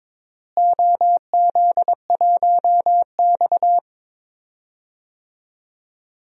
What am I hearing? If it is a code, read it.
Morse code, heard as a keyed tone, about 22 wpm, 700 Hz. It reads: OZ1X
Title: DXCC-100 Call Signs - 4 Characters - Repeated 2x - 22wpm